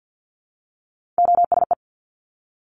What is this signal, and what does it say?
Morse code, heard as a keyed tone, about 50 wpm, 700 Hz. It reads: OSE
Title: Top Three Letter Combinations in Words 50wpm